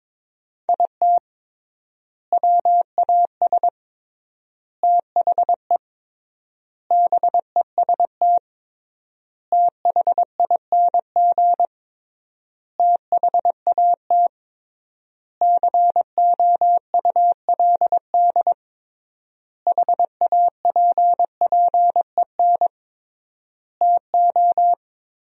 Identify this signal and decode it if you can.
Morse code, heard as a keyed tone, about 22 wpm, 700 Hz. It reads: IT WAS THE BEST THING THAT COULD HAPPEN TO
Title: 3x Word-Spacing - Sentences from Top 400 Words - 22wpm